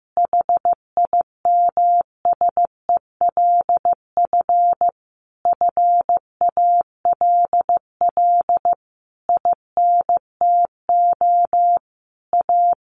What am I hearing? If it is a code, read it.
Morse code, heard as a keyed tone, about 15 wpm, 700 Hz. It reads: HIMSELF FALL INTO A
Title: A Sound Of Thunder 15wpm